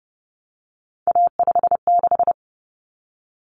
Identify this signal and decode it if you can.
Morse code, heard as a keyed tone, about 30 wpm, 700 Hz. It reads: A56
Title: Common Cut Numbers - 3-Digit 30wpm